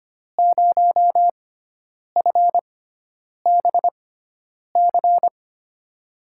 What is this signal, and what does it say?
Morse code, heard as a keyed tone, about 25 wpm, spaced as for 9 wpm, 700 Hz. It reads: 0FBC
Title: USA Call Signs - 5 Characters - 6x Character Spacing - 25wpm